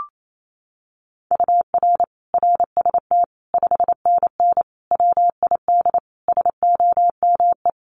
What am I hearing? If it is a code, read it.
Morse code, heard as a keyed tone, about 28 wpm, 700 Hz. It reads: UR RST 5NN WID SOME
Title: Common QSO Phrases 28wpm